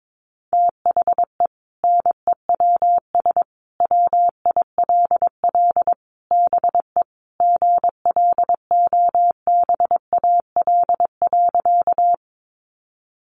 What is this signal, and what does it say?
Morse code, heard as a keyed tone, about 22 wpm, 700 Hz. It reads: THE NEWS WILL BE GLOBAL.
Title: Sentences from Top 1200 Words 22wpm